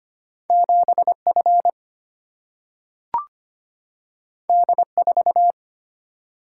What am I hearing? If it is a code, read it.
Morse code, heard as a keyed tone, about 25 wpm, 700 Hz. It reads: 7 F D 4
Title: Single-Letter Single-Number 25wpm